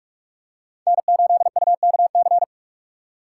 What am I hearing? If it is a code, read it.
Morse code, heard as a keyed tone, about 42 wpm, 700 Hz. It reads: N8UKC
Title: USA Call Signs - 5 Characters - 4x Character Spacing - 45wpm